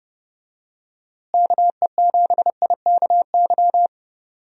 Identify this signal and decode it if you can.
Morse code, heard as a keyed tone, about 29 wpm, 700 Hz. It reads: KE7IKY
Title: USA Call Signs - 6 Characters - Repeated 2x - 30wpm